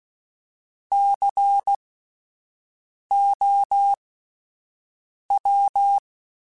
Morse code, heard as a keyed tone, about 16 wpm, spaced as for 5 wpm, 800 Hz. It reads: COW